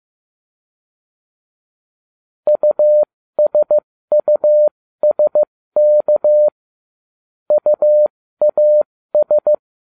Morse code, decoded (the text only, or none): USUSK UAS